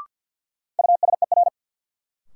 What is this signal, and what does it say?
Morse code, heard as a keyed tone, about 53 wpm, 700 Hz. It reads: USER
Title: Sentences from Top 1000 Words - Review Part 5 - New Word Drill - 50wpm